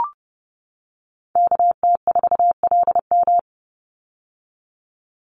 Morse code, heard as a keyed tone, about 30 wpm, 700 Hz. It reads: KT4LM
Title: USA Call Signs 30wpm